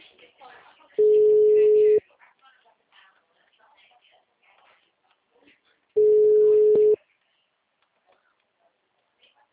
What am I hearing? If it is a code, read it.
Ringback tone (Europe, 425 Hz), with speech.
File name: out- -2008- - - .50824